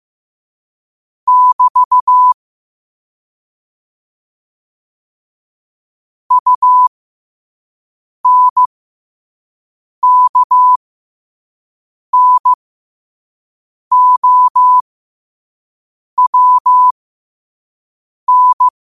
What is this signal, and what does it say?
Morse code, heard as a keyed tone, about 15 wpm, spaced as for 5 wpm, 1000 Hz. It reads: <BT> UNKNOWN
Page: New quotes every day in morse code at 5 Words per minute.